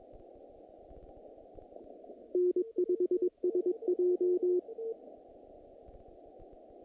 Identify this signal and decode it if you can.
Morse code, heard as a keyed tone, about 22 wpm, 360 Hz. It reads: N5SJ
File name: contests / naqp-cw-k5tr-op-k5pi- -2100